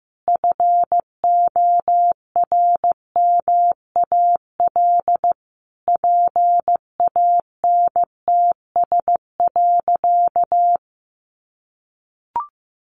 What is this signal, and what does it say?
Morse code, heard as a keyed tone, about 15 wpm, 700 Hz. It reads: FORMAL PANTS.
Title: Additional Easy Phrases 15wpm